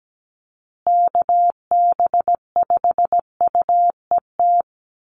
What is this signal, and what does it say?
Morse code, heard as a keyed tone, about 17 wpm, 700 Hz. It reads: KB5UET
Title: USA Call Signs Repeated 3x 17wpm